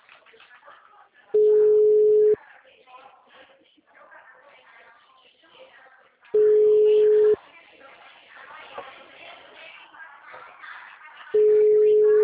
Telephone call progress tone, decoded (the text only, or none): ringback tone